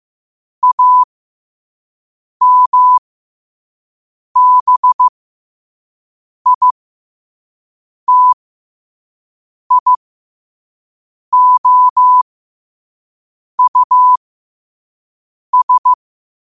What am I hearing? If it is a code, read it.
Morse code, heard as a keyed tone, about 15 wpm, spaced as for 5 wpm, 1000 Hz. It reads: AMBITIOUS